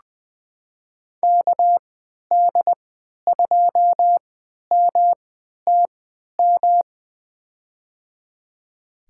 Morse code, heard as a keyed tone, about 20 wpm, spaced as for 11 wpm, 700 Hz. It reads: KD2MTM
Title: USA Call Signs - 6 Characters - 3x Character Spacing - 20wpm